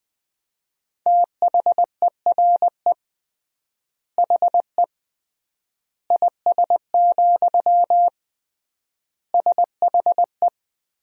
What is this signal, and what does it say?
Morse code, heard as a keyed tone, about 20 wpm, 700 Hz. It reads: THERE HE IS, SHE
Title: 3x Word-Spacing - Sentences from Top 100 Words - 20wpm